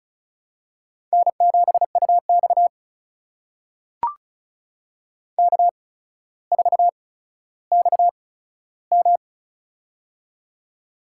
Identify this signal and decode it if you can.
Morse code, heard as a keyed tone, about 36 wpm, spaced as for 15 wpm, 700 Hz. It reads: N7UX K4XM
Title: USA Call Signs - 4 Characters - 8x Character Spacing - 35wpm